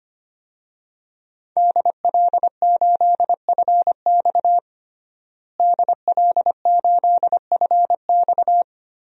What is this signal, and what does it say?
Morse code, heard as a keyed tone, about 25 wpm, 700 Hz. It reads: DL8FX DL8FX
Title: DXCC-100 Call Signs - 5 Characters - Repeated 3x - 25wpm